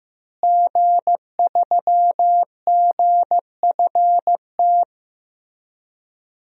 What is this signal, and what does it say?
Morse code, heard as a keyed tone, about 15 wpm, 700 Hz. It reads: G3GFT
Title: DXCC-100 Call Signs - 5 Characters - Repeated 2x - 15wpm